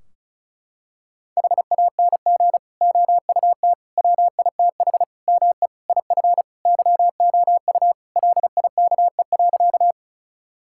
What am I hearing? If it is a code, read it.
Morse code, heard as a keyed tone, about 34 wpm, 700 Hz. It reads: HANG OUT WITH ME IF YOU LIKE.
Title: Sentences from Top 1600 Words 35wpm